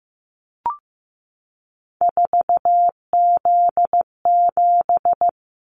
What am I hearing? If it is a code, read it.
Morse code, heard as a keyed tone, about 15 wpm, 700 Hz. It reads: 4Z7
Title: Callsign Prefixes for Top 50 DXCC - 15wpm